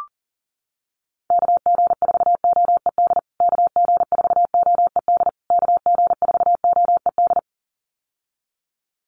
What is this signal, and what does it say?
Morse code, heard as a keyed tone, about 40 wpm, 700 Hz. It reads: KG4OED KG4OED KG4OED
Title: USA Call Signs Repeated 3x 40wpm